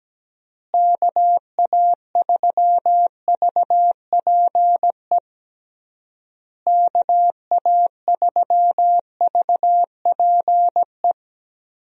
Morse code, heard as a keyed tone, about 17 wpm, 700 Hz. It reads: KA3VPE KA3VPE
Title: USA Call Signs - 6 Characters - Repeated 3x - 17wpm